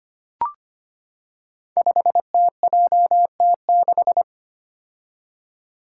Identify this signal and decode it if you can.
Morse code, heard as a keyed tone, about 25 wpm, 700 Hz. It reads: 5TJT6
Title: Five Challenging Letter-Number Combinations 25wpm